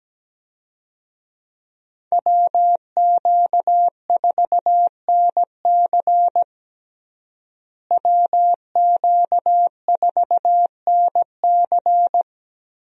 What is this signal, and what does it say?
Morse code, heard as a keyed tone, about 17 wpm, 700 Hz. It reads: WQ4NC WQ4NC
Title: USA Call Signs - 5 Characters - Repeated 3x - 17wpm